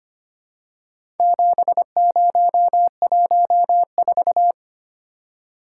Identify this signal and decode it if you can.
Morse code, heard as a keyed tone, about 25 wpm, 700 Hz. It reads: 7014
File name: practice / 25wpm / Four Letter-Number Combinations - 25wpm